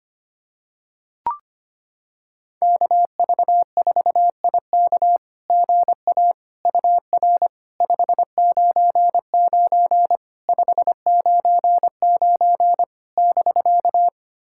Morse code, heard as a keyed tone, about 25 wpm, 700 Hz. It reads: KV4IK GA UR 599 599 <BK>
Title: SOTA QSO Practice 25wpm